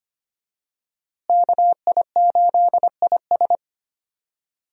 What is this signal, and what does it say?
Morse code, heard as a keyed tone, about 25 wpm, 700 Hz. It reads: KI8IS
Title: USA Call Signs - 5 Characters - Repeated 3x - 25wpm